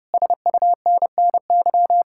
Morse code, heard as a keyed tone, about 36 wpm, 700 Hz. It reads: SUNNY
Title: Speed-Racing Common QSO Phrases - 30wpm